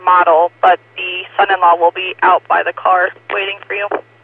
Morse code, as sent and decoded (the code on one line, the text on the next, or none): none